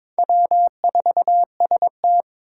Morse code, heard as a keyed tone, about 22 wpm, 700 Hz. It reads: W4ST
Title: Four Challenging Letter-Number Combinations 22wpm